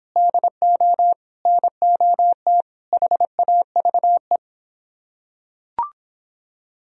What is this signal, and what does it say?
Morse code, heard as a keyed tone, about 24 wpm, 700 Hz. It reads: DO NOT HAVE